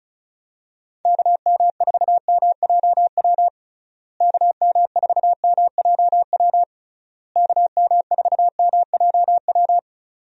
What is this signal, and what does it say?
Morse code, heard as a keyed tone, about 34 wpm, 700 Hz. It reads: KM4MJW KM4MJW KM4MJW
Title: USA Call Signs - 6 Characters - Repeated 3x - 35wpm